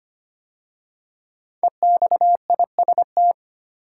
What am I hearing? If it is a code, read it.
Morse code, heard as a keyed tone, about 25 wpm, 700 Hz. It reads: EXIST
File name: 5 Letter Words - First 3 Letters Given - 25wpm